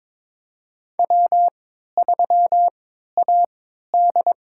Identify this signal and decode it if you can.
Morse code, heard as a keyed tone, about 22 wpm, spaced as for 13 wpm, 700 Hz. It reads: W3AD
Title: USA Call Signs - 4 Characters - 3x Character Spacing - 22wpm